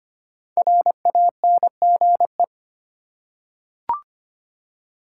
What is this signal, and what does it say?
Morse code, heard as a keyed tone, about 25 wpm, 700 Hz. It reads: RANGE
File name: 5 Letter Words - First 3 Letters Given - 25wpm